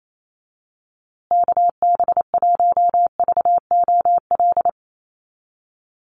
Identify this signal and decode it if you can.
Morse code, heard as a keyed tone, about 28 wpm, 700 Hz. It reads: KB1VOL
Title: USA Call Signs Repeated 3x 28wpm